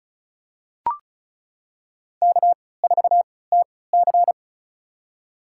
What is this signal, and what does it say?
Morse code, heard as a keyed tone, about 36 wpm, spaced as for 20 wpm, 700 Hz. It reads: K4TC
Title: USA Call Signs - 3x Character Spacing - 35wpm